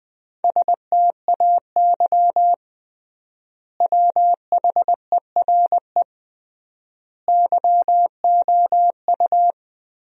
Morse code, heard as a keyed tone, about 20 wpm, 700 Hz. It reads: STAY WHERE YOU
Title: 3x Word-Spacing - Sentences from Top 500 Words - 20wpm